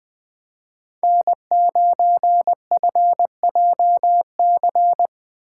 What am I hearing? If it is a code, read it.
Morse code, heard as a keyed tone, about 20 wpm, 700 Hz. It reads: N9FJC